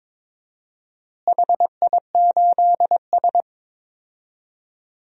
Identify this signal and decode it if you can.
Morse code, heard as a keyed tone, about 22 wpm, 700 Hz. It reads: HI8S
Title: DXCC-100 Call Signs - 4 Characters - Repeated 3x - 22wpm